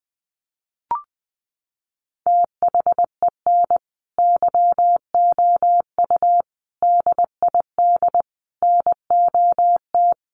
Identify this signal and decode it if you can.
Morse code, heard as a keyed tone, about 20 wpm, 700 Hz. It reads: THEN YOU DID NOT
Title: Sentences from Top 600 Words 20wpm